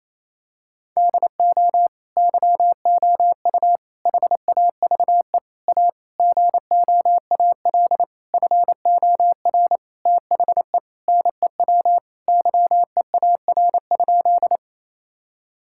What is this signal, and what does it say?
Morse code, heard as a keyed tone, about 28 wpm, 700 Hz. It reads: DO YOU HAVE A GOAL FOR THE NEW YEAR?
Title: Sentences from Top 1000 Words - Review Part 5 - 28wpm